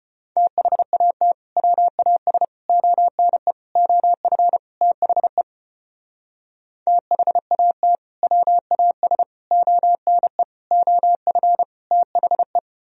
Morse code, heard as a keyed tone, about 32 wpm, 700 Hz. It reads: THAT WAS ONE OF THE THAT WAS ONE OF THE